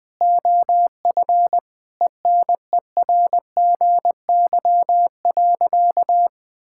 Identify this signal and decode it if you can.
Morse code, heard as a keyed tone, about 20 wpm, 700 Hz. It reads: OF ENERGY.
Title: Sentences from Top 1000 Words - Review Part 6 - 20wpm